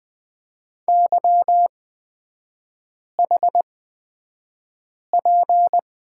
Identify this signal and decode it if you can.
Morse code, heard as a keyed tone, about 20 wpm, spaced as for 5 wpm, 700 Hz. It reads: YHP
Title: 224 Lesson 36e - BT - Three Characters 20wpm